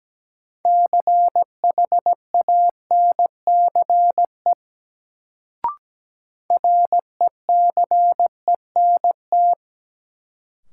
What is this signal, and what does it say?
Morse code, heard as a keyed tone, about 17 wpm, 700 Hz. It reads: CHANCE RECENT
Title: Sets of 3 Words - New Word Drill - 17wpm